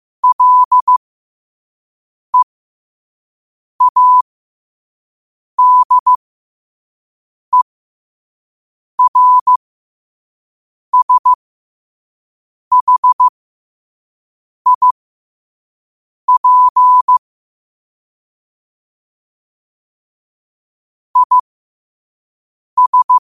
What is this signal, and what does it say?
Morse code, heard as a keyed tone, about 15 wpm, spaced as for 5 wpm, 1000 Hz. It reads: LEADERSHIP IS
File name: qotdfiles / 5wpm